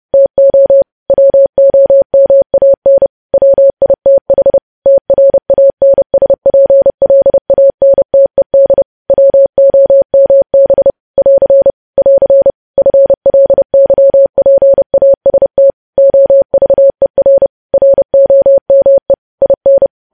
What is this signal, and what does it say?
Morse code, heard as a keyed tone, about 30 wpm, 550 Hz. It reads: TO WOMAN WITH TRANSPLANTED WOMB <AR> <AR> FLYPAST OVER ROME IN
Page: news headlines in morse code at 30 wpm